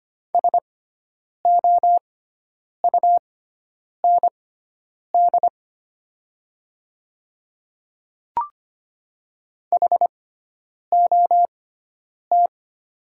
Morse code, heard as a keyed tone, about 25 wpm, spaced as for 9 wpm, 700 Hz. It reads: SOUND HOT
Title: 6x Character Spacing - Top 200 Words - 25wpm